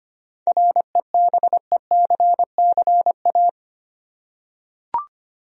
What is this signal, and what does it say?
Morse code, heard as a keyed tone, about 25 wpm, 700 Hz. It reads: REBECCA
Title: USA Common Names - 25wpm